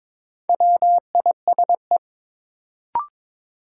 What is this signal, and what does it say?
Morse code, heard as a keyed tone, about 21 wpm, 700 Hz. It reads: WISE
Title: Sentences from Top 1600 Words - New Word Drill - 22wpm